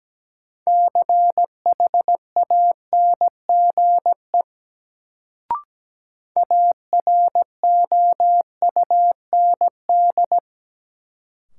Morse code, heard as a keyed tone, about 17 wpm, 700 Hz. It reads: CHANGE AROUND
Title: Sentences from Top 600 Words - New Word Drill - 17wpm